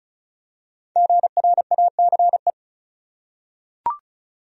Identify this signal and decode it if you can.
Morse code, heard as a keyed tone, about 36 wpm, 700 Hz. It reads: GRACE
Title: USA Common Names - 35wpm